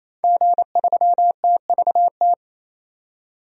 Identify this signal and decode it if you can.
Morse code, heard as a keyed tone, about 28 wpm, 700 Hz. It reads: G3TVT
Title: DXCC-100 Call Signs - 5 Characters - 28wpm